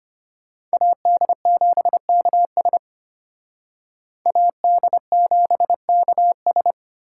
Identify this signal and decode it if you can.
Morse code, heard as a keyed tone, about 32 wpm, 700 Hz. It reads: AD7KS AD7KS